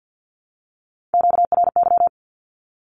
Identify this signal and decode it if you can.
Morse code, heard as a keyed tone, about 48 wpm, 700 Hz. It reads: QRY
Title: Q-Codes 50wpm